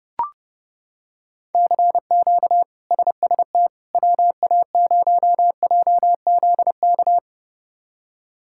Morse code, heard as a keyed tone, about 32 wpm, 700 Hz. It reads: CQ SST WA0JZK
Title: K1USN SST QSO Practice - 30wpm